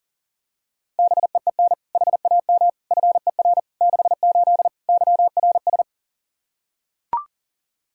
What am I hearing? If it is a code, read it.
Morse code, heard as a keyed tone, about 40 wpm, 700 Hz. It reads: BEEN HAM FER 68 YRS
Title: K1USN SST QSO Practice 40wpm